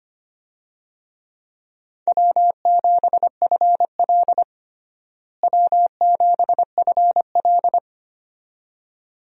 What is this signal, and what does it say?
Morse code, heard as a keyed tone, about 25 wpm, 700 Hz. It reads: W7FL W7FL
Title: USA Call Signs - 4 Characters - Repeated 2x - 25wpm